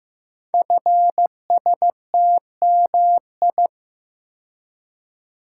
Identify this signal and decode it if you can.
Morse code, heard as a keyed tone, about 15 wpm, 700 Hz. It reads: FSTMI
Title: Code Groups - 5 Letters - 15wpm